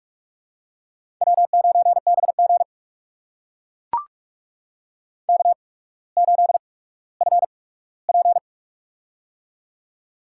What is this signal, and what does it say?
Morse code, heard as a keyed tone, about 44 wpm, spaced as for 19 wpm, 700 Hz. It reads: W9BG K8FP
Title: USA Call Signs - 4 Characters - 8x Character Spacing - 45wpm